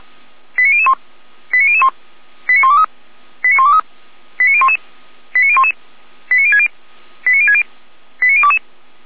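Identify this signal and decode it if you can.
Five-tone selective call (ZVEI), heard as quick runs of five tones, tones of about 70 ms. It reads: 89001 89001 89123 89123 89011 89011 89060 89060 89022
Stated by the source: ZVEI five-tone selective call received on an SDR, archived by feuerwehr-orlamuende.org.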